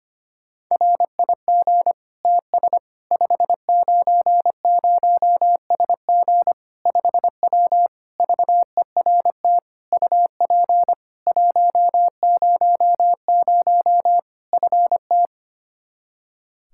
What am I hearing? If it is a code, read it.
Morse code, heard as a keyed tone, about 25 wpm, 700 Hz. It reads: RIG TS 590SG 5W VERT UP 100 FT
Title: Practice QSO - Exchange 2 - Info - 25wpm